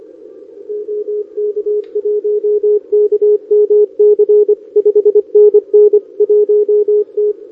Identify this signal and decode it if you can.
Morse code, heard as a keyed tone, about 24 wpm, 410 Hz. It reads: OK1KMC 5NN 1T
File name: OK QRP 2026 / OKQRP2026